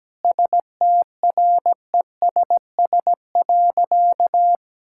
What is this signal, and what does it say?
Morse code, heard as a keyed tone, about 17 wpm, 700 Hz. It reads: STRESS.
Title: Sentences from Top 1000 Words - Review Part 6 - 17wpm